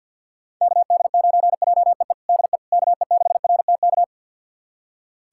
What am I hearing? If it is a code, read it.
Morse code, heard as a keyed tone, about 51 wpm, 700 Hz. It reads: KD9JEE DE KE6RTK
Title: USA Callsign DE Callsign - 6 Characters - 50wpm